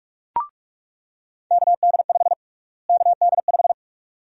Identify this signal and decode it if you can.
Morse code, heard as a keyed tone, about 42 wpm, 700 Hz. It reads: KD5 KD5
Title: USA Call Sign Prefixes - Repeated 2x - 45wpm